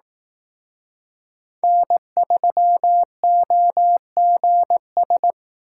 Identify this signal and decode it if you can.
Morse code, heard as a keyed tone, about 18 wpm, 700 Hz. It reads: N3OGS
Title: Speed Racing - USA 1x3 Call Signs 15wpm